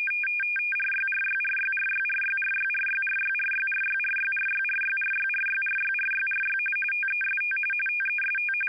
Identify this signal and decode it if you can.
RTTY, about 45.45 baud. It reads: RYRYRYRYRYRYRYRYRYRYRYRYRYRYRYRYRYRY TEST TEST
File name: Audio Files / All Good Men 60 wpm RTTY 850 Hz